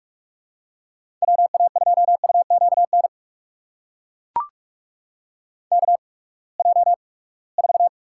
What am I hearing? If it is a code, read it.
Morse code, heard as a keyed tone, about 42 wpm, spaced as for 25 wpm, 700 Hz. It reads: WA2UQN KJ4
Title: USA Call Signs - 8x Character Spacing - 45wpm